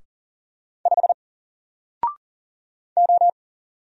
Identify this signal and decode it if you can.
Morse code, heard as a keyed tone, about 39 wpm, 700 Hz. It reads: 5 O
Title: Single Letter-Number Alt 40wpm